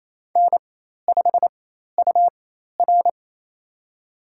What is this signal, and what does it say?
Morse code, heard as a keyed tone, about 28 wpm, spaced as for 13 wpm, 700 Hz. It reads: N5UR
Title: USA Call Signs - 4 Characters - 4x Character Spacing - 28wpm